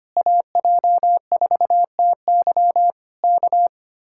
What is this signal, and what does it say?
Morse code, heard as a keyed tone, about 24 wpm, 700 Hz. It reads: AJ4TY K
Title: Speed-Racing Common QSO Phrases - 25wpm